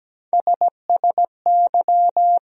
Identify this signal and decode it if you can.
Morse code, heard as a keyed tone, about 17 wpm, 700 Hz. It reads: SSY
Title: USA Call Sign Postfixes - 3 Letters 17wpm